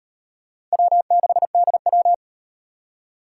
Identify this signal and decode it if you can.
Morse code, heard as a keyed tone, about 38 wpm, 700 Hz. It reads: W6DW